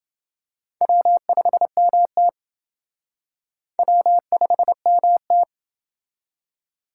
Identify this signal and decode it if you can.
Morse code, heard as a keyed tone, about 28 wpm, 700 Hz. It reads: W5MT W5MT